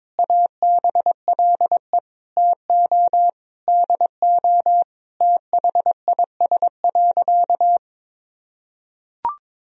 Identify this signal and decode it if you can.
Morse code, heard as a keyed tone, about 22 wpm, 700 Hz. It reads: ABLE TO DO THIS.
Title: Sentences from Top 1000 Words - Review Part 2 - 22wpm